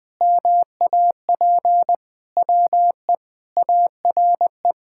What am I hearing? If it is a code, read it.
Morse code, heard as a keyed tone, about 21 wpm, 700 Hz. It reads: MAP WE ARE
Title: Sentences with 2-Word Phrases - 20wpm